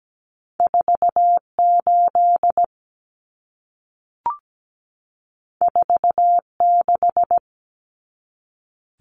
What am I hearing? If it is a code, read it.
Morse code, heard as a keyed tone, about 17 wpm, 700 Hz. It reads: 48 46
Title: Common Cut Numbers - 2-Digit 17wpm